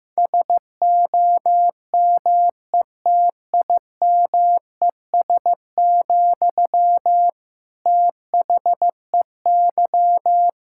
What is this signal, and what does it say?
Morse code, heard as a keyed tone, about 15 wpm, 700 Hz. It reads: SOMETIMES, THEY
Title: Sentences from Top 1000 Words - Review Part 6 - 15wpm